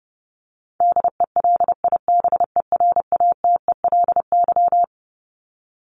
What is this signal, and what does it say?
Morse code, heard as a keyed tone, about 30 wpm, 700 Hz. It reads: DELIBERATELY
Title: Top 1751-2000 Words 30wpm